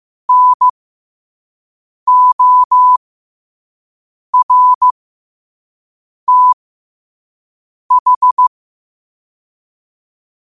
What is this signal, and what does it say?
Morse code, heard as a keyed tone, about 15 wpm, spaced as for 5 wpm, 1000 Hz. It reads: NORTH
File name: qotdfiles / 5wpm